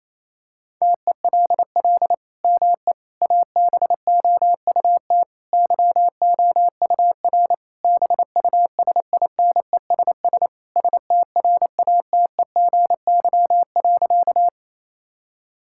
Morse code, heard as a keyed tone, about 28 wpm, 700 Hz. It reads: TELL ME ABOUT YOUR BUSINESS STRATEGY.
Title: Sentences from Top 1000 Words - Review Part 5 - 28wpm